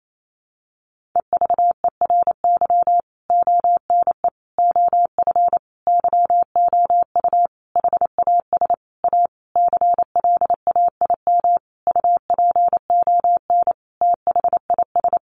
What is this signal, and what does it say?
Morse code, heard as a keyed tone, about 28 wpm, 700 Hz. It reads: EVERY ONE OF YOU HAS A CLAIM UPON THIS
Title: Sentences from Top 1500 Words 28wpm